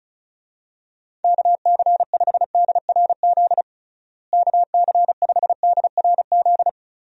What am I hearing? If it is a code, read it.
Morse code, heard as a keyed tone, about 36 wpm, 700 Hz. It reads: KC5DRZ KC5DRZ
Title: USA Call Signs - 6 Characters - Repeated 2x - 35wpm